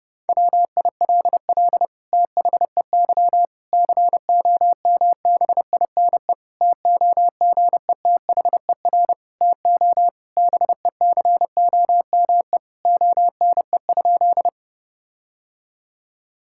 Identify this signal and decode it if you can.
Morse code, heard as a keyed tone, about 30 wpm, 700 Hz. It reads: WILL THEY COMBINE TOGETHER TO BECOME ONE?
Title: Sentences from Top 1600 Words 30wpm